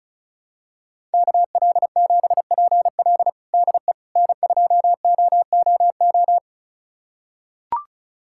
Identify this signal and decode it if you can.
Morse code, heard as a keyed tone, about 34 wpm, 700 Hz. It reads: KL7PL DE N2OOO
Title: USA Callsign DE Callsign - 5 Characters - 35wpm